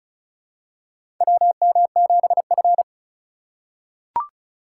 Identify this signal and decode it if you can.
Morse code, heard as a keyed tone, about 36 wpm, 700 Hz. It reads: WM7F
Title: USA Call Signs - 4x Character Spacing - 35wpm